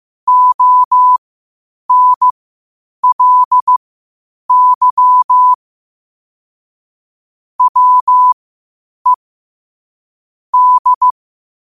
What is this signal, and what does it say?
Morse code, heard as a keyed tone, about 15 wpm, spaced as for 8 wpm, 1000 Hz. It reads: ONLY WE D
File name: qotdfiles / 8wpm